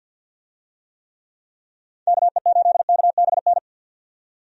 Morse code, heard as a keyed tone, about 50 wpm, 700 Hz. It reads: KE8KBN
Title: USA Call Signs - 6 Characters - 50wpm